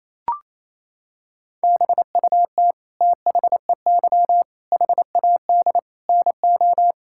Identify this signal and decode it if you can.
Morse code, heard as a keyed tone, about 27 wpm, 700 Hz. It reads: BUT THEY HAD NO
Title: Sentences from Top 1000 Words - Review Part 6 - 28wpm